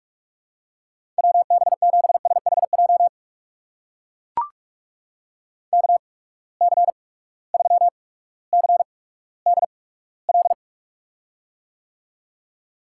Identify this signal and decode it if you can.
Morse code, heard as a keyed tone, about 44 wpm, spaced as for 19 wpm, 700 Hz. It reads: WB7SHJ KC3CDL
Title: USA Call Signs - 6 Characters - 8x Character Spacing - 45wpm